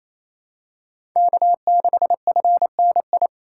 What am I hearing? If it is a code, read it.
Morse code, heard as a keyed tone, about 28 wpm, 700 Hz. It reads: K6FNI
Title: USA Call Signs - 5 Characters - 4x Character Spacing - 28wpm